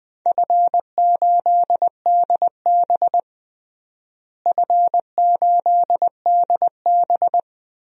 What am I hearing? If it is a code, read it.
Morse code, heard as a keyed tone, about 20 wpm, 700 Hz. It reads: F8DB F8DB
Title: DXCC-100 Call Signs - 4 Characters - Repeated 3x - 20wpm